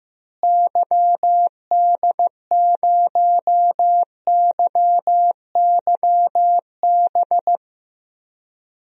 Morse code, heard as a keyed tone, about 15 wpm, 700 Hz. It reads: YD0YYB
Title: DXCC-100 Call Signs - 6 Characters - Repeated 3x - 15wpm